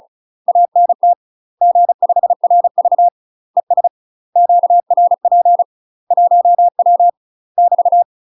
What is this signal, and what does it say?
Morse code, heard as a keyed tone, about 35 wpm, 700 Hz. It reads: ANT G5RV ES QRP 1W <BT>
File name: UK MASTER QSOs TRAINING / 35WPM / UK MASTER QSOs 0025